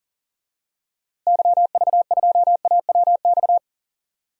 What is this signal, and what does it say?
Morse code, heard as a keyed tone, about 38 wpm, 700 Hz. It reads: YV2AWX
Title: DXCC-100 Call Signs - 6 Characters - Repeated 2x - 40wpm